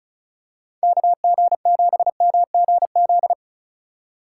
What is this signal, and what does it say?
Morse code, heard as a keyed tone, about 36 wpm, 700 Hz. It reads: KG7MGZ
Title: USA Call Signs - 6 Characters - 3x Character Spacing - 35wpm